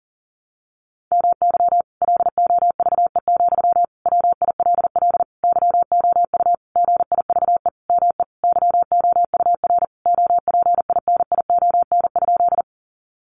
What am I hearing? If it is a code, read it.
Morse code, heard as a keyed tone, about 40 wpm, 700 Hz. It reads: MY LOVE, WILL YOU GIVE ME YOUR OPINION?
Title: Sentences from Top 1200 Words 40wpm